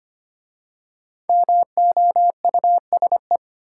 Morse code, heard as a keyed tone, about 25 wpm, 700 Hz. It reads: MOUSE
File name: practice / 25wpm / Easy Words with Hints - 25wpm